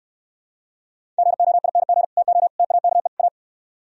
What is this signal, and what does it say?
Morse code, heard as a keyed tone, about 67 wpm, 700 Hz. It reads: GOING TO TAKE A